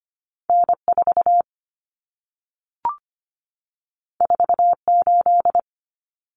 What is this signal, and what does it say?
Morse code, heard as a keyed tone, about 25 wpm, 700 Hz. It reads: N4 48
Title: Common Cut Numbers - 2-Digit 25wpm